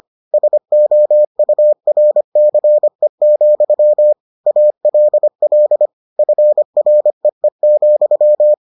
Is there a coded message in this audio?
Yes: Morse code from 0.3 to 8.5 s, heard as a keyed tone, about 25 wpm, 600 Hz. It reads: SOURCE, ALL FREE,